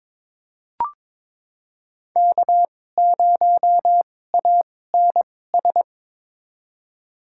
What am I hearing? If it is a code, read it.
Morse code, heard as a keyed tone, about 22 wpm, spaced as for 16 wpm, 700 Hz. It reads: K0ANS
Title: USA Call Signs - 5 Characters - 2x Character Spacing - 22wpm